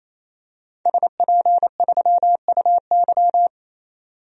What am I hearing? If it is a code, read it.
Morse code, heard as a keyed tone, about 28 wpm, 700 Hz. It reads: SP3UY
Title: DXCC-100 Call Signs - 5 Characters - 28wpm